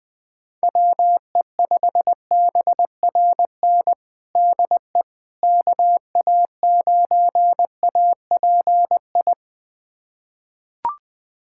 Morse code, heard as a keyed tone, about 20 wpm, 700 Hz. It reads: WE5BRN DE KA9API
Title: USA Callsign DE Callsign - 6 Characters - 20wpm